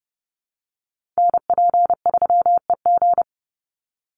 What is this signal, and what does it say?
Morse code, heard as a keyed tone, about 30 wpm, 700 Hz. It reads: NP3EG
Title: USA Call Signs 30wpm